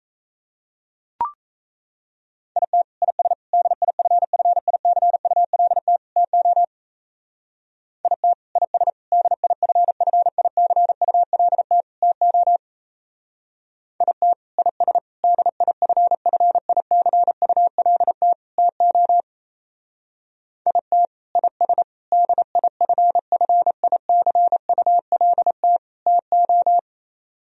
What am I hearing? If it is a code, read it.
Morse code, heard as a keyed tone, about 35 wpm, 700 Hz. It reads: IT IS DIFFICULT TO IT IS DIFFICULT TO IT IS DIFFICULT TO IT IS DIFFICULT TO